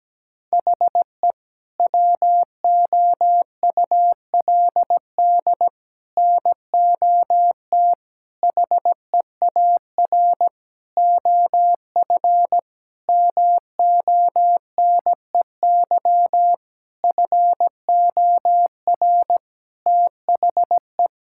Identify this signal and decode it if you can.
Morse code, heard as a keyed tone, about 17 wpm, 700 Hz. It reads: HE WOULD NOT HEAR OF MONEY FOR THE
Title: Sentences from Top 1000 Words - Review Part 5 - 17wpm